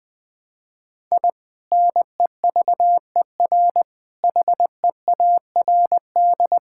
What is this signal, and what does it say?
Morse code, heard as a keyed tone, about 20 wpm, 700 Hz. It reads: I NEVER HEARD
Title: Sentences from Top 1000 Words - Review Part 5 - 20wpm